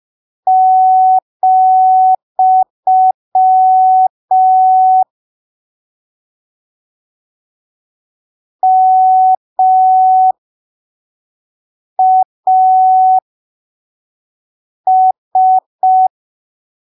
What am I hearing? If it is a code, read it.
Morse code, heard as a keyed tone, about 5 wpm, spaced as for 3 wpm, 750 Hz. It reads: , MAS
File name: arquivos / texto / texto 02 5ppm